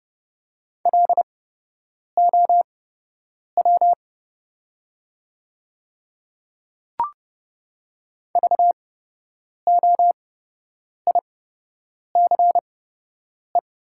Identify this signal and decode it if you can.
Morse code, heard as a keyed tone, about 30 wpm, spaced as for 8 wpm, 690 Hz. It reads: LOW VOICE
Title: 8x Character Spacing - Top 500 Words - 30wpm